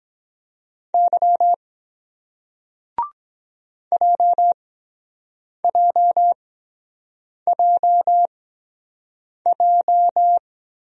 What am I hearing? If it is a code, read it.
Morse code, heard as a keyed tone, about 23 wpm, spaced as for 7 wpm, 700 Hz. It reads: Y JJJJ